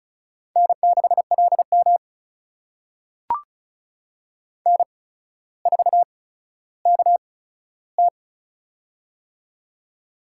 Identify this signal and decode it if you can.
Morse code, heard as a keyed tone, about 36 wpm, spaced as for 15 wpm, 700 Hz. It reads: N6LM N4KT
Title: USA Call Signs - 4 Characters - 8x Character Spacing - 35wpm